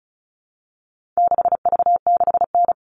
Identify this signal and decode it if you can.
Morse code, heard as a keyed tone, about 34 wpm, 700 Hz. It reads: 6V6N
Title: All Cut Numbers - 4-Digit 35wpm